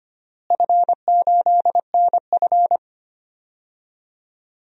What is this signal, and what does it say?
Morse code, heard as a keyed tone, about 25 wpm, 700 Hz. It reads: F8NF
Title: DXCC-100 Call Signs - 4 Characters - Repeated 3x - 25wpm